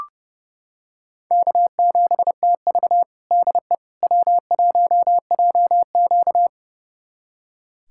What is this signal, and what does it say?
Morse code, heard as a keyed tone, about 30 wpm, 700 Hz. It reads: K7TV DE W1JQ
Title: USA Callsign DE Callsign - 4 Characters - 30wpm